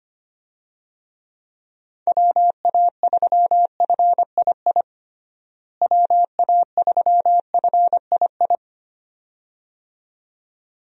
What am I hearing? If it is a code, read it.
Morse code, heard as a keyed tone, about 25 wpm, 700 Hz. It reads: WA3FII WA3FII
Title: USA Call Signs - 6 Characters - Repeated 2x - 25wpm